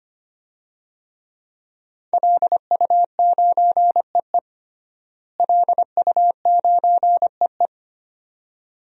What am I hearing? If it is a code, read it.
Morse code, heard as a keyed tone, about 25 wpm, 700 Hz. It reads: LU9EE LU9EE
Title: DXCC-100 Call Signs - 5 Characters - Repeated 2x - 25wpm